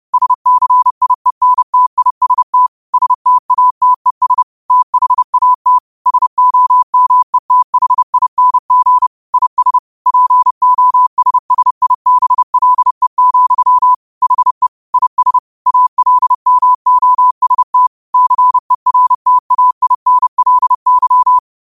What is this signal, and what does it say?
Morse code, heard as a keyed tone, about 30 wpm, 1000 Hz. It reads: SCIENTIST STATES THAT SOMETHING IS POSSIBLE, HE IS ALMOST CERTAINLY
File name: qotdfiles / 30wpm